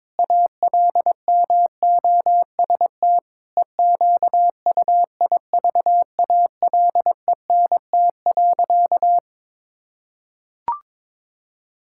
Morse code, heard as a keyed tone, about 22 wpm, 700 Hz. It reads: ALMOST EQUIVALENT.
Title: Sentences from Top 1600 Words 22wpm